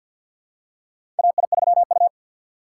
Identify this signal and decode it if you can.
Morse code, heard as a keyed tone, about 53 wpm, 700 Hz. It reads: AI3U
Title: USA Call Signs - 6x Character Spacing - 50wpm